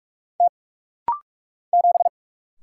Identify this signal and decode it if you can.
Morse code, heard as a keyed tone, about 46 wpm, 700 Hz. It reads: T 7
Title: Single Letter-Number - Rapid-Fire 45wpm